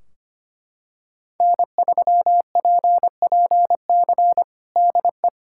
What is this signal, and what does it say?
Morse code, heard as a keyed tone, about 25 wpm, 700 Hz. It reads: N3PPC DE
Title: K1USN SST QSO Practice 25wpm